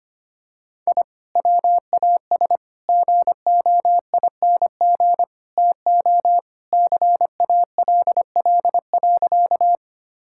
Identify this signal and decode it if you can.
Morse code, heard as a keyed tone, about 25 wpm, 700 Hz. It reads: I WAS GOING TO CALL.
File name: practice / 25wpm / Sentences with 4-Word Phrases - 25wpm